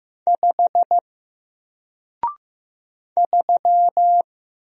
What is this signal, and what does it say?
Morse code, heard as a keyed tone, about 15 wpm, 700 Hz. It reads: 5 3
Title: Single Letter-Number Alt 15wpm